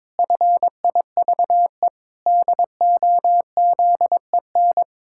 Morse code, heard as a keyed tone, about 21 wpm, 700 Hz. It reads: FIVE DOZEN
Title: Pangram Challenge v2 - 22wpm